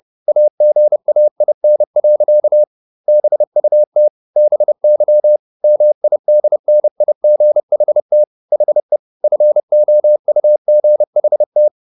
Morse code, heard as a keyed tone, about 30 wpm, 600 Hz. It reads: AGAIN. BUT BY MIDNIGHT HE FOUGHT